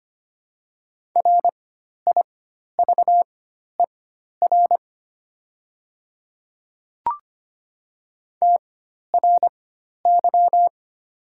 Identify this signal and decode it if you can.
Morse code, heard as a keyed tone, about 25 wpm, spaced as for 12 wpm, 700 Hz. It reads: RIVER TRY
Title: 4x Character Spacing - Top 300 Words - 25wpm